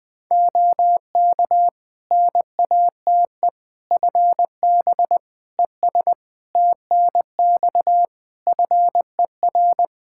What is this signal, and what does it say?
Morse code, heard as a keyed tone, about 20 wpm, 700 Hz. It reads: OK NATE FB ES TNX FER
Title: Practice QSO - Exchange 2 - Info - 20wpm